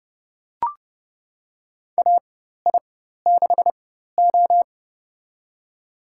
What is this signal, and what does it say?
Morse code, heard as a keyed tone, about 30 wpm, spaced as for 14 wpm, 700 Hz. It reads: AI6O
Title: USA Call Signs - 4 Characters - 4x Character Spacing - 30wpm